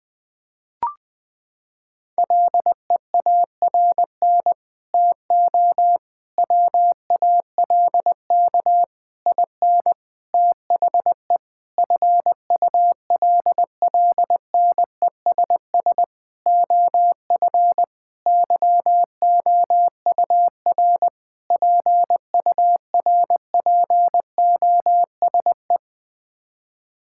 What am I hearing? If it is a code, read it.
Morse code, heard as a keyed tone, about 20 wpm, 700 Hz. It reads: LEARN TO WALK IN THE FULLNESS OF YOUR PURPOSE
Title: Sets of 9 Words - 20wpm